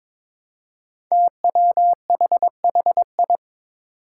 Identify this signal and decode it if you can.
Morse code, heard as a keyed tone, about 22 wpm, 700 Hz. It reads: TWHHI
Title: Five Challenging Letter-Number Combinations 22wpm